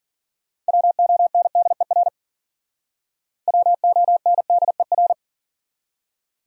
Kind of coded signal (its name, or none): Morse code